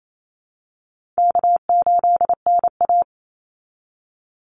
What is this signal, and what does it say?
Morse code, heard as a keyed tone, about 28 wpm, 700 Hz. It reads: K8NA
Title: Easy Call Signs Repeated 3x - 28wpm